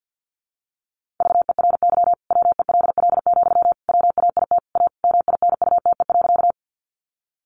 Morse code, heard as a keyed tone, about 50 wpm, 700 Hz. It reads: VERY WELL, WAIT A MINUTE.
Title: Sentences from Top 500 Words 50wpm